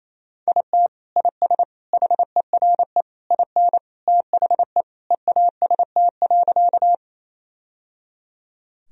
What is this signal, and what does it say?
Morse code, heard as a keyed tone, about 28 wpm, 700 Hz. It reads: IT IS HERE IN THE EAST.
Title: Sentences from Top 1000 Words - Review Part 3 - 28wpm